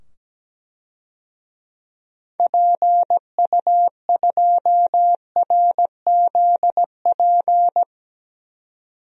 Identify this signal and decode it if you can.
Morse code, heard as a keyed tone, about 17 wpm, 700 Hz. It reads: PU2RZP
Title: DXCC-100 Call Signs - 6 Characters - 17wpm